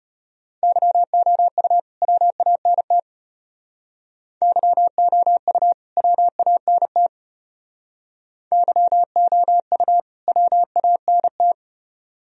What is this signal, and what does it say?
Morse code, heard as a keyed tone, about 33 wpm, 700 Hz. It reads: YOU WANT YOU WANT YOU WANT